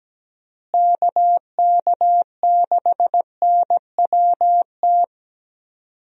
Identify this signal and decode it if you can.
Morse code, heard as a keyed tone, about 17 wpm, 700 Hz. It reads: KK6NWT
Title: USA Call Signs - 6 Characters - Repeated 2x - 17wpm